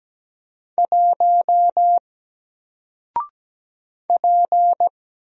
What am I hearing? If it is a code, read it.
Morse code, heard as a keyed tone, about 17 wpm, 700 Hz. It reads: 1 P
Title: Single Letter-Number Alt 17wpm